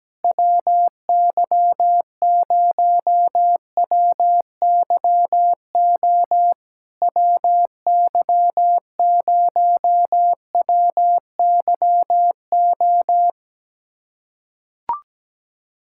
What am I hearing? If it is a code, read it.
Morse code, heard as a keyed tone, about 17 wpm, 690 Hz. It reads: WY0WYO WY0WYO
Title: SOTA QSO Practice 17wpm